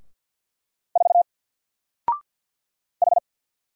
Morse code, heard as a keyed tone, about 53 wpm, 700 Hz. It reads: 4 H
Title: Single Letter-Number 50wpm